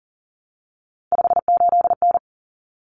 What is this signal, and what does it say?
Morse code, heard as a keyed tone, about 40 wpm, 700 Hz. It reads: 58N
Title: Common Cut Numbers - 3-Digit 40wpm